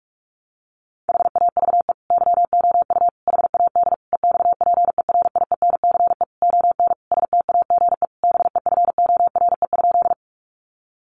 Morse code, heard as a keyed tone, about 44 wpm, 700 Hz. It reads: HAVE YOU HAD EXPERIENCE ON STAGE BEFORE?
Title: Sentences from Top 1200 Words 45wpm